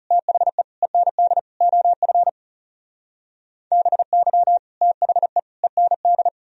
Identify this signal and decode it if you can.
Morse code, heard as a keyed tone, about 37 wpm, 700 Hz. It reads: THE END OF BY THE END
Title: Speed-Racing - Sets of 4 Words 30wpm